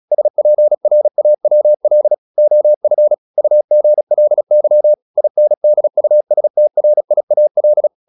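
Morse code, heard as a keyed tone, about 36 wpm, 600 Hz. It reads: SPRAWL OF UGLY INDUSTRIAL
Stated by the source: Morse practice at 36 wpm, text of Ken Follett's novel Never, from CW med Gnister.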